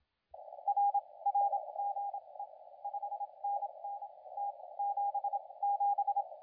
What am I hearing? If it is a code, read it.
Morse code, heard as a keyed tone, about 27 wpm, spaced as for 21 wpm, 810 Hz, with radio noise. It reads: RR GE 5NN T77
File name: la4c / sac-cw-2017 / CONTEST AUDIO 2100Z